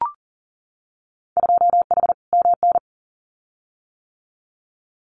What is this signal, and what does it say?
Morse code, heard as a keyed tone, about 38 wpm, 700 Hz. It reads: 2H MN
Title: Winter Field Day Exchanges 40wpm